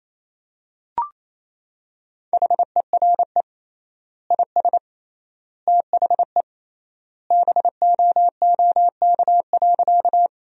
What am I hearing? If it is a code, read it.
Morse code, heard as a keyed tone, about 28 wpm, 700 Hz. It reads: HERE IS THE BOOK.
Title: 3x Word-Spacing - Sentences from Top 300 Words - 28wpm